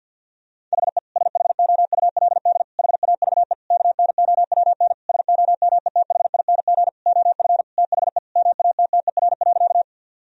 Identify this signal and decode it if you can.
Morse code, heard as a keyed tone, about 52 wpm, 700 Hz. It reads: HE SHOULD HAVE KNOWN SOMETHING OF THE MATTER.
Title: Sentences from Top 1000 Words - Review Part 3 - 50wpm